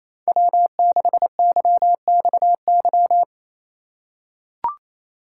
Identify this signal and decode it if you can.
Morse code, heard as a keyed tone, about 28 wpm, 700 Hz. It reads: W6YXY
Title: USA Call Signs - 2x Character Spacing - 28wpm